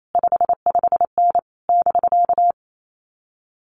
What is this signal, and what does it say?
Morse code, heard as a keyed tone, about 28 wpm, 700 Hz. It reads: 55N <BK>
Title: Common QSO Phrases Encore 28wpm